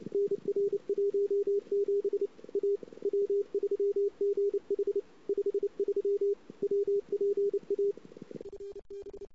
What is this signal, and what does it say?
Morse code, heard as a keyed tone, about 31 wpm, 400 Hz. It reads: NR 17 A W3GH 53 WPA FB
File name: CW / 2008-ss-cw-k5tr-op-n5rz -19:00